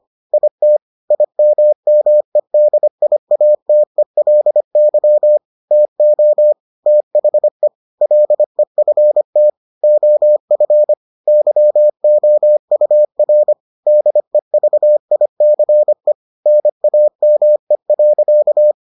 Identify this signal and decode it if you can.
Morse code, heard as a keyed tone, about 25 wpm, 600 Hz. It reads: IT IMMEDIATELY TO THE LEFT OF YOUR DEVICE NAME.